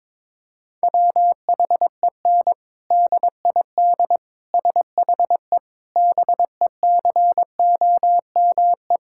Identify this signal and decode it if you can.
Morse code, heard as a keyed tone, about 21 wpm, 700 Hz. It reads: WHEN DID SHE BECOME
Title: Sentences from Top 1000 Words - Review Part 6 - 22wpm